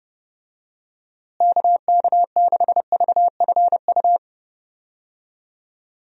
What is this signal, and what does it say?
Morse code, heard as a keyed tone, about 30 wpm, 700 Hz. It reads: KK6VFU
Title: USA Call Signs - 6 Characters - Repeated 3x - 30wpm